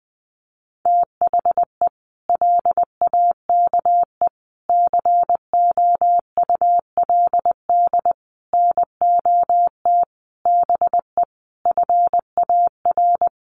Morse code, heard as a keyed tone, about 20 wpm, 700 Hz. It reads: THE LAKE COULD NOT BE FAR
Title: Sentences from Top 1100 Words 20wpm